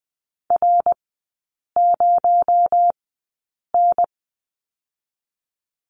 Morse code, heard as a keyed tone, about 20 wpm, spaced as for 8 wpm, 700 Hz. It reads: R0N
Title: 218 Lesson 35e - Number 0 - Three Characters 20wpm